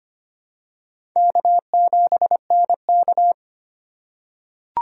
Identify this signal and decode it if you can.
Morse code, heard as a keyed tone, about 25 wpm, 700 Hz. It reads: K7NK
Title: USA Call Signs - 4 Characters - 2x Character Spacing - 25wpm